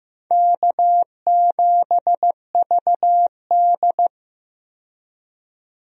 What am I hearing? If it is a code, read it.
Morse code, heard as a keyed tone, about 15 wpm, 700 Hz. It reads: K7VD
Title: USA Call Signs - 4 Characters - 15wpm